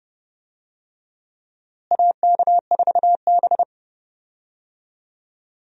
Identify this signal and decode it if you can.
Morse code, heard as a keyed tone, about 32 wpm, 700 Hz. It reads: AK4B
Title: USA Call Signs - 4 Characters - Repeated 3x - 30wpm